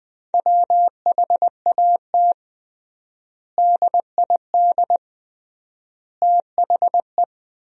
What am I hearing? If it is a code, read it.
Morse code, heard as a keyed tone, about 20 wpm, 700 Hz. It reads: WHAT DID THE
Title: 3x Word-Spacing - Sentences from Top 500 Words - 20wpm